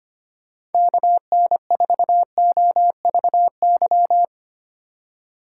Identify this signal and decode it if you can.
Morse code, heard as a keyed tone, about 32 wpm, 700 Hz. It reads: KN4OVY